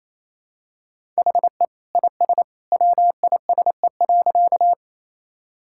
Morse code, heard as a keyed tone, about 28 wpm, 700 Hz. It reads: HE IS WISE.
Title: Sentences from Top 1600 Words 28wpm